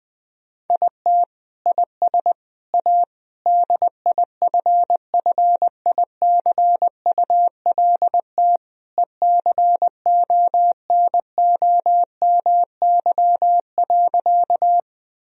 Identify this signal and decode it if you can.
Morse code, heard as a keyed tone, about 20 wpm, 700 Hz. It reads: IT IS A DIFFICULT ECONOMY.
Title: Sentences from Top 1000 Words - Review Part 3 - 20wpm